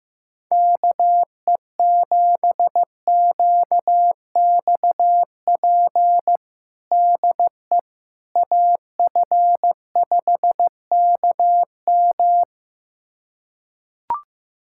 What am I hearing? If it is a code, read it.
Morse code, heard as a keyed tone, about 15 wpm, 700 Hz. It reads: KE7QXP DE AF5KM
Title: USA Callsign DE Callsign 15wpm